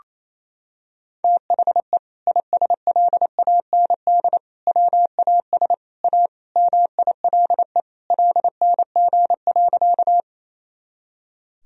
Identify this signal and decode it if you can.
Morse code, heard as a keyed tone, about 28 wpm, 700 Hz. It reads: THE ISLAND WAS A MILE LNG.
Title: Sentences from Top 500 Words and Abbreviations 28wpm